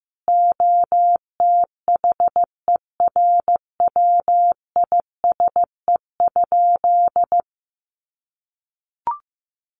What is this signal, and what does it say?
Morse code, heard as a keyed tone, about 15 wpm, 700 Hz. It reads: OTHERWISE?
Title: Sentences from Top 1200 Words 15wpm